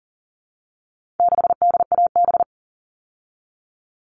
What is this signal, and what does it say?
Morse code, heard as a keyed tone, about 40 wpm, 700 Hz. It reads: 6DAB
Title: All Cut Numbers - 4-Digit 40wpm